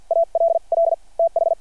Morse code, heard as a keyed tone, about 52 wpm, spaced as for 41 wpm, 660 Hz. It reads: ALL TH
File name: archived / CRSnet